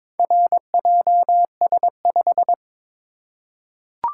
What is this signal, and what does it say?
Morse code, heard as a keyed tone, about 21 wpm, 690 Hz. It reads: RJS5
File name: practice / 22wpm / Four Letter-Number Combinations - 22wpm